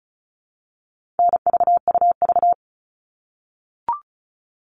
Morse code, heard as a keyed tone, about 34 wpm, 700 Hz. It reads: NVUV
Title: All Cut Numbers - 4-Digit 35wpm